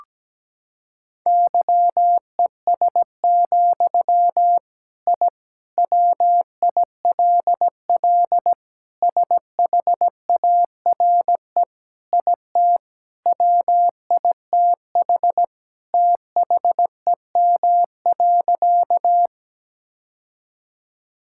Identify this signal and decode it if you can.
Morse code, heard as a keyed tone, about 17 wpm, 700 Hz. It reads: YES, I WILL SHARE IT WITH THEM.
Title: Sentences from Top 1000 Words - Review Part 6 - 17wpm